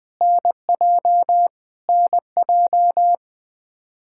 Morse code, heard as a keyed tone, about 21 wpm, 700 Hz. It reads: NJ NJ
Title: Practice QSO - Exchange 1 - Beginning - 20wpm